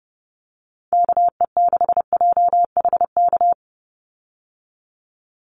Morse code, heard as a keyed tone, about 30 wpm, 700 Hz. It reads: KE6JHK
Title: USA Call Signs 30wpm